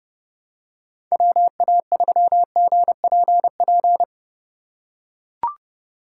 Morse code, heard as a keyed tone, about 31 wpm, 700 Hz. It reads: WA3GPP
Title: K1USN SST QSO Practice - 30wpm